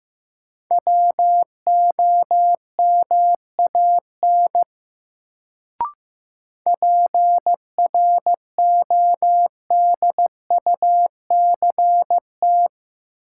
Morse code, heard as a keyed tone, about 15 wpm, 700 Hz. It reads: WOMAN PRODUCT
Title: Sentences from Top 1000 Words - Review Part 2 - New Word Drill - 15wpm